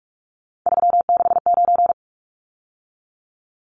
Morse code, heard as a keyed tone, about 44 wpm, 700 Hz. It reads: 369
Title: Numbers - 3-Digit 45wpm